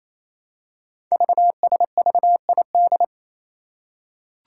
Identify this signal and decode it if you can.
Morse code, heard as a keyed tone, about 28 wpm, 700 Hz. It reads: VSVID
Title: Code Groups - 5 Letters - 28wpm